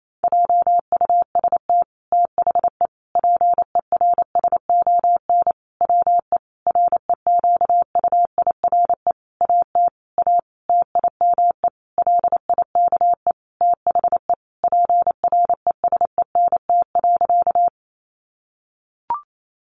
Morse code, heard as a keyed tone, about 28 wpm, 690 Hz. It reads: JUST THE PERSON WE REQUIRE AT A TIME LIKE THE PRESENT.
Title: Sentences from Top 900 Words 28wpm